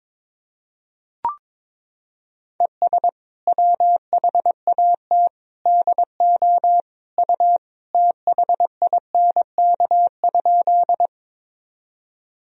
Morse code, heard as a keyed tone, about 22 wpm, 700 Hz. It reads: ES WHAT DO U THINK?
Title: Sentences from Top 200 Words and Abbreviations 22wpm